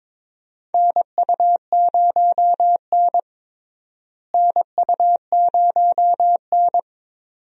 Morse code, heard as a keyed tone, about 22 wpm, 700 Hz. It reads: NU0N NU0N
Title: USA Call Signs - 4 Characters - Repeated 2x - 22wpm